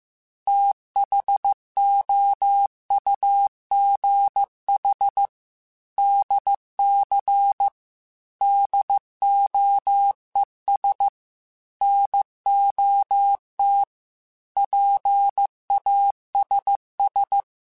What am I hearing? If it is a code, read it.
Morse code, heard as a keyed tone, about 15 wpm, 800 Hz. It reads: THOUGH DC DOES NOT PASS
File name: Technical Subjects / Technical Subjects-mp3s / Sw Pwr Sup @13